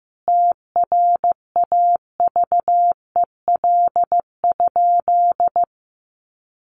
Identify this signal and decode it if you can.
Morse code, heard as a keyed tone, about 15 wpm, 700 Hz. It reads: TRAVEL?
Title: Sentences from Top 1400 Words 15wpm